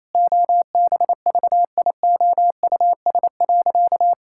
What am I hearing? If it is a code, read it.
Morse code, heard as a keyed tone, about 27 wpm, 700 Hz. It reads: OBVIOUS.
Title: Sentences from Top 1600 Words 28wpm